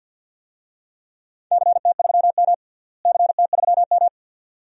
Morse code, heard as a keyed tone, about 50 wpm, 700 Hz. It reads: CT3M CT3M
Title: DXCC-100 Call Signs - 4 Characters - Repeated 2x - 50wpm